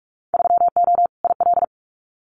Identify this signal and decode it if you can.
Morse code, heard as a keyed tone, about 46 wpm, 700 Hz. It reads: 3O IL
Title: Winter Field Day Exchanges 45wpm